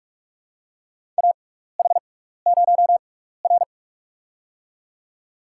Morse code, heard as a keyed tone, about 44 wpm, spaced as for 15 wpm, 700 Hz. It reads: AH0R
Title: USA Call Signs - 4 Characters - 6x Character Spacing - 45wpm